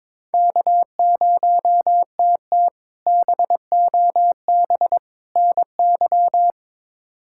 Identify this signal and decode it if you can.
Morse code, heard as a keyed tone, about 22 wpm, 700 Hz. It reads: K0TT BOB NY
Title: K1USN SST QSO Practice - 22wpm